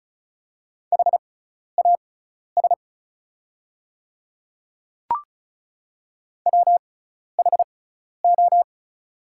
Morse code, heard as a keyed tone, about 36 wpm, spaced as for 12 wpm, 700 Hz. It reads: HAS WHO
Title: 6x Character Spacing - Top 100 Words - 35wpm